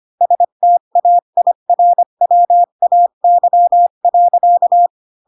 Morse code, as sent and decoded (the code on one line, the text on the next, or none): ... - .- .. .-. .-- .- -.-- .-.-.-
STAIRWAY.